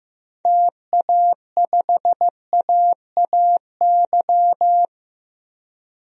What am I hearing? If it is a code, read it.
Morse code, heard as a keyed tone, about 15 wpm, 700 Hz. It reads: TA5AAY
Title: DXCC-100 Call Signs - 6 Characters - Repeated 2x - 15wpm